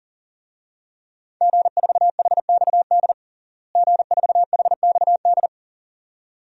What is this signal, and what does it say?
Morse code, heard as a keyed tone, about 37 wpm, 700 Hz. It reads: G4HXD G4HXD
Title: DXCC-100 Call Signs - 5 Characters - Repeated 2x - 40wpm